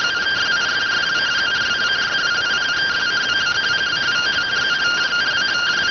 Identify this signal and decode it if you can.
Two-tone teleprinter signal, about 170 Hz shift.